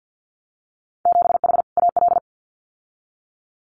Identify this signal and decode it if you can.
Morse code, heard as a keyed tone, about 50 wpm, 700 Hz. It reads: 7H AL
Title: Winter Field Day Exchanges Encore 50wpm